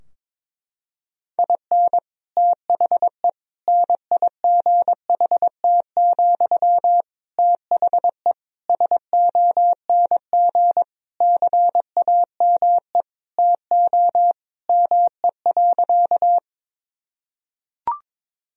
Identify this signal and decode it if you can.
Morse code, heard as a keyed tone, about 22 wpm, 700 Hz. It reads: IN THE NIGHT, THE SONG CAME TO ME.
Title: Sentences from Top 1000 Words - Review Part 2 - 22wpm